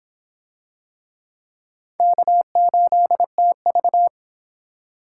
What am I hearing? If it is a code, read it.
Morse code, heard as a keyed tone, about 25 wpm, 700 Hz. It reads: K8TV